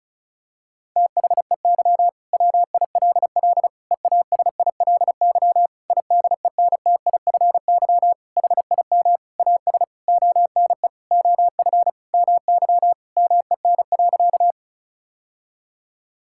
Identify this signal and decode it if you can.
Morse code, heard as a keyed tone, about 36 wpm, 700 Hz. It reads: THEY WILL EASILY IDENTIFY HIM AS ONE OF MY MEN.
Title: Sentences from Top 1000 Words - Review Part 6 - 35wpm